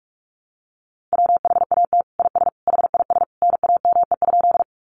Morse code, heard as a keyed tone, about 44 wpm, 700 Hz. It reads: WHAT IS HIS NAME?
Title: Sentences from Top 200 Words 45wpm